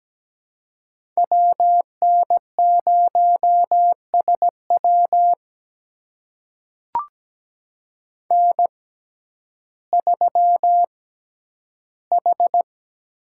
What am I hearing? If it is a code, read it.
Morse code, heard as a keyed tone, about 17 wpm, spaced as for 10 wpm, 700 Hz. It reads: W N0SW N3H
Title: USA Call Signs - 5 Characters - 6x Character Spacing - 17wpm